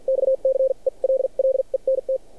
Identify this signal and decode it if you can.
Morse code, heard as a keyed tone, about 47 wpm, spaced as for 38 wpm, 530 Hz. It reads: XCELLENT
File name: archived / CRSnet